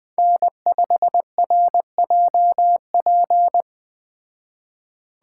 Morse code, heard as a keyed tone, about 20 wpm, 700 Hz. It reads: N5RJP